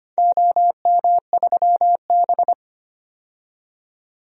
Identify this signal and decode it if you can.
Morse code, heard as a keyed tone, about 25 wpm, 690 Hz. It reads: OM3B
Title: DXCC-100 Call Signs - 4 Characters - 25wpm